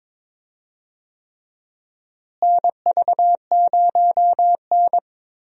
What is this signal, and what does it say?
Morse code, heard as a keyed tone, about 21 wpm, 700 Hz. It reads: NV0N
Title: USA Call Signs - 4 Characters - Repeated 3x - 22wpm